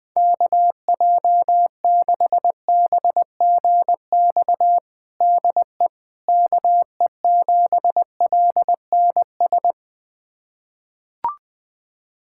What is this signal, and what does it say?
Morse code, heard as a keyed tone, about 20 wpm, 700 Hz. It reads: KJ6BGX DE KE7LNS
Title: USA Callsign DE Callsign - 6 Characters - 20wpm